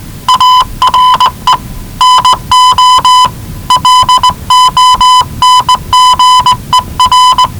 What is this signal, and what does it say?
Morse code, heard as a keyed tone, about 20 wpm, 1000 Hz. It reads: ARE NO LONGER
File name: WellingtonRadioZlwCloseDownOnHfAndMf / 01-Zlw 4277khz 417.5 qrt until 0925z